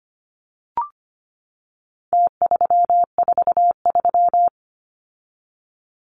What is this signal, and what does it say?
Morse code, heard as a keyed tone, about 25 wpm, 700 Hz. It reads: T343
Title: Numbers - Contest Style - 4-Digit 25wpm